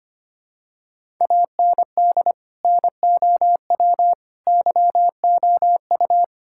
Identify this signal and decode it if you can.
Morse code, heard as a keyed tone, about 25 wpm, 700 Hz. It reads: AND NOW YOU
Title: Sentences from Top 1000 Words - Review Part 6 - 25wpm